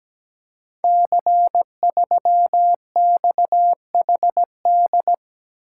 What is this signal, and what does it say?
Morse code, heard as a keyed tone, about 17 wpm, 700 Hz. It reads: C3XHD
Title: Code Groups - 5 Letter:Numbers - 17wpm